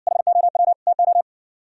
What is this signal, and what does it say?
Morse code, heard as a keyed tone, about 60 wpm, 700 Hz. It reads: HOW TO